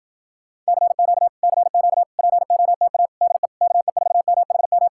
Morse code, heard as a keyed tone, about 52 wpm, 700 Hz. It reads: CQ CQ POTA DE KE4MHM